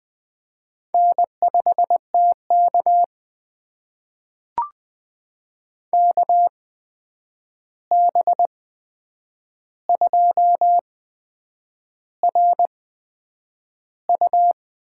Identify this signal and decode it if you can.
Morse code, heard as a keyed tone, about 20 wpm, spaced as for 8 wpm, 700 Hz. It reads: N5TK KB2RU
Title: USA Call Signs - 8x Character Spacing - 20wpm